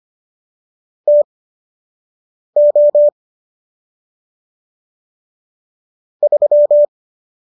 Morse code, heard as a keyed tone, about 25 wpm, spaced as for 6 wpm, 590 Hz. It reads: TO 3